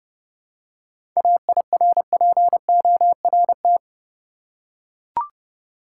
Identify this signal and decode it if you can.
Morse code, heard as a keyed tone, about 30 wpm, 700 Hz. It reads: AIRPORT
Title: Seven Letter Words - 30wpm